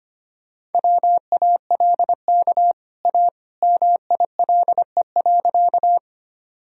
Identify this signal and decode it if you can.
Morse code, heard as a keyed tone, about 25 wpm, 700 Hz. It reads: WALK A MILE.
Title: Sentences from Top 1000 Words - Review Part 1 - 25wpm